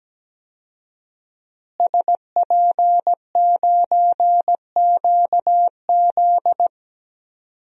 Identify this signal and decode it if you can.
Morse code, heard as a keyed tone, about 17 wpm, 700 Hz. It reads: SP9QZ
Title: DXCC-100 Call Signs - 5 Characters - Repeated 3x - 17wpm